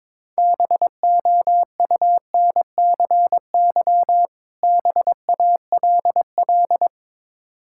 Morse code, heard as a keyed tone, about 22 wpm, 700 Hz. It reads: BOUNCY BALL
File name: Difficult Words with Hints - 22wpm